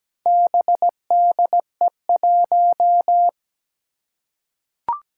Morse code, heard as a keyed tone, about 17 wpm, 700 Hz. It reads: BDE1
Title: Four Challenging Letter-Number Combinations 17wpm